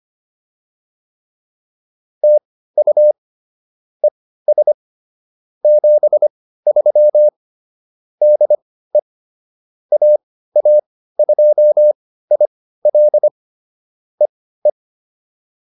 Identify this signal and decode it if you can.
Morse code, heard as a keyed tone, about 25 wpm, spaced as for 15 wpm, 600 Hz. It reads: TU ES 73 DE AA2IL EE